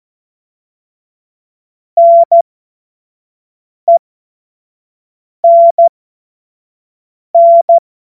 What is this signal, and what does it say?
Morse code, heard as a keyed tone, about 14 wpm, spaced as for 5 wpm, 680 Hz. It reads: NENN
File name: Morsekurs / Witze / Witz 30. Dezember 2025 25 F70